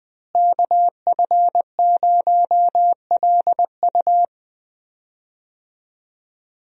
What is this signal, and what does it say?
Morse code, heard as a keyed tone, about 21 wpm, 700 Hz. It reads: KF0LU
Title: USA Call Signs - 5 Characters - Repeated 3x - 20wpm